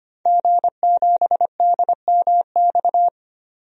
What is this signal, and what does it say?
Morse code, heard as a keyed tone, about 25 wpm, 700 Hz. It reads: G7DMX
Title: DXCC-100 Call Signs - 5 Characters - 25wpm